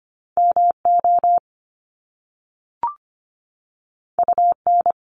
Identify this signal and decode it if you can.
Morse code, heard as a keyed tone, about 25 wpm, 700 Hz. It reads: MO UN
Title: Top Two Letter Combinations in Words 25wpm